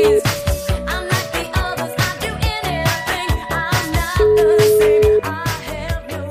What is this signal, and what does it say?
Ringback tone (Europe, 425 Hz), with noise.